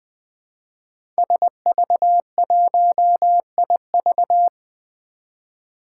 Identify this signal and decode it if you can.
Morse code, heard as a keyed tone, about 20 wpm, 700 Hz. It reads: SV1IV
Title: Five Challenging Letter-Number Combinations 20wpm